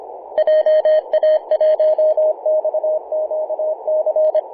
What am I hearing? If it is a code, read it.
Morse code, heard as a keyed tone, about 24 wpm, 620 Hz, with radio noise. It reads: JA1XQC
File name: Apr/29 0606z VK9NT 50.115MHz CW